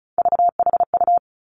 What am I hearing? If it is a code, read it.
Morse code, heard as a keyed tone, about 36 wpm, 700 Hz. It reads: VHU
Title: Three Challenging Letter-Number Combinations 35wpm